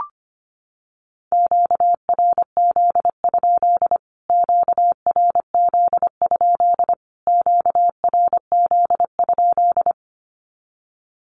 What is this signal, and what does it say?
Morse code, heard as a keyed tone, about 25 wpm, 700 Hz. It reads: QRZ? QRZ? QRZ?
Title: Q-Codes Repeated 3x 25wpm